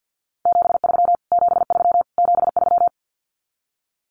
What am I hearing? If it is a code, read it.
Morse code, heard as a keyed tone, about 50 wpm, 700 Hz. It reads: 73 73 73
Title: Abbreviations Repeated 3x - 50wpm